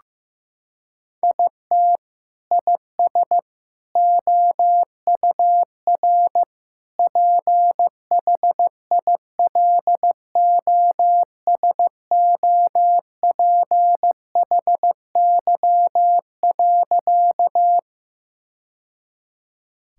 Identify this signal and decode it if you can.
Morse code, heard as a keyed tone, about 16 wpm, 700 Hz. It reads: IT IS OUR PHILOSOPHY.
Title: Sentences from Top 1000 Words - Review Part 6 - 15wpm